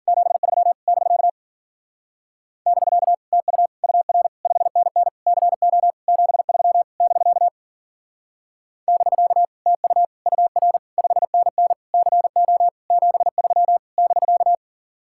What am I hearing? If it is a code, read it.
Morse code, heard as a keyed tone, about 53 wpm, 700 Hz. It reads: 73 <BK> <BK> TU UR 5NN CO 73 <BK> <BK> TU UR 5NN CO 73 <BK>